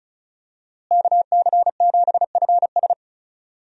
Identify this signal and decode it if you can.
Morse code, heard as a keyed tone, about 36 wpm, 700 Hz. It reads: KC7FS
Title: USA Call Signs - 5 Characters - Repeated 3x - 35wpm